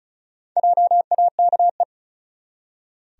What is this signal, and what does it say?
Morse code, heard as a keyed tone, about 34 wpm, 700 Hz. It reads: JAKE
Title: USA Common Names - 35wpm